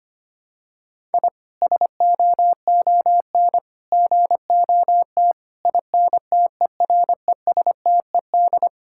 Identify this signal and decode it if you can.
Morse code, heard as a keyed tone, about 25 wpm, 700 Hz. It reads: I SOON GOT INTERESTED
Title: Sentences from Top 1000 Words - Review Part 6 - 25wpm